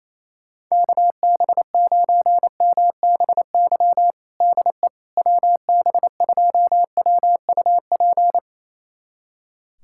Morse code, heard as a keyed tone, about 27 wpm, 700 Hz. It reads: KB9MBY DE WB2WUP
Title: USA Callsign DE Callsign - 6 Characters - 28wpm